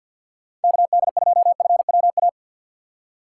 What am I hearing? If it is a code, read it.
Morse code, heard as a keyed tone, about 52 wpm, 700 Hz. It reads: KD2FWA
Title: USA Call Signs - 4x Character Spacing - 50wpm